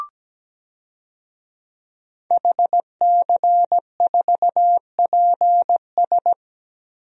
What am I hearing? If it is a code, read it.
Morse code, heard as a keyed tone, about 17 wpm, 700 Hz. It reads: HC4PS
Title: DXCC-100 Call Signs - 5 Characters - Repeated 3x - 17wpm